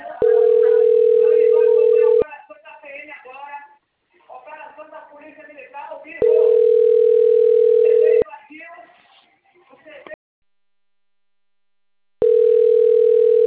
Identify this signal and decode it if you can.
Ringback tone (North America), 440 + 480 Hz, with speech.
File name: rg-1600-unknown- - - .26659